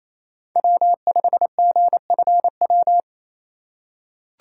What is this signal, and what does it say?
Morse code, heard as a keyed tone, about 28 wpm, 690 Hz. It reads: W5GFW
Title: USA Call Signs - 5 Characters - 4x Character Spacing - 28wpm